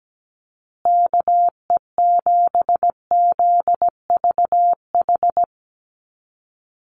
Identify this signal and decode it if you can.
Morse code, heard as a keyed tone, about 17 wpm, 700 Hz. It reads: KE7ZVH
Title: USA Call Signs Repeated 3x 17wpm